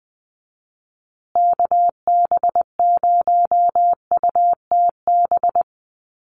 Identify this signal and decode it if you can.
Morse code, heard as a keyed tone, about 20 wpm, 700 Hz. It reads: KB0UTB
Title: USA Call Signs Repeated 3x 20wpm